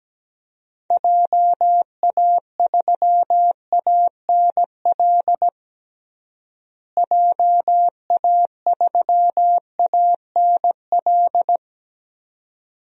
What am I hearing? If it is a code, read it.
Morse code, heard as a keyed tone, about 17 wpm, 700 Hz. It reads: JA3ANL JA3ANL
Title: DXCC-100 Call Signs - 6 Characters - Repeated 2x - 17wpm